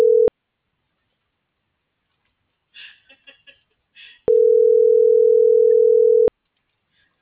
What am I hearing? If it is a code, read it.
Ringback tone (North America), 440 + 480 Hz.
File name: CENTRAL MULTIMODAL / out alfredovelazquez - .81546